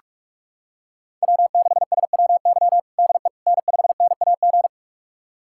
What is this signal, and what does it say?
Morse code, heard as a keyed tone, about 47 wpm, 700 Hz. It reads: W6SWY DE N5NAG
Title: USA Callsign DE Callsign 45wpm